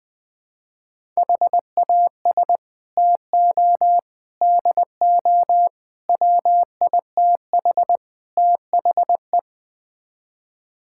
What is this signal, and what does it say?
Morse code, heard as a keyed tone, about 20 wpm, 700 Hz. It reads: HAS TO DO WITH THE
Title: Build A Phrase - 5-Word Phrases - 20wpm